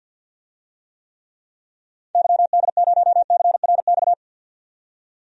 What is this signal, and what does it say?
Morse code, heard as a keyed tone, about 47 wpm, 700 Hz. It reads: YD0CRX
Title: DXCC-100 Call Signs - 6 Characters - 50wpm